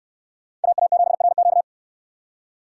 Morse code, heard as a keyed tone, about 68 wpm, 690 Hz. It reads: AA7RQ